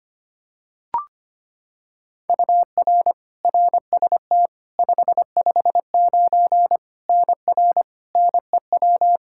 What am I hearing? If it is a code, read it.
Morse code, heard as a keyed tone, about 25 wpm, 700 Hz. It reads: UR RST 559 NR NEW
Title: K1USN SST QSO Practice 25wpm